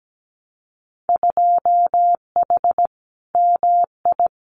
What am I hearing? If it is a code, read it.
Morse code, heard as a keyed tone, about 17 wpm, 700 Hz. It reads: 2H MI
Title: Winter Field Day Exchanges 17wpm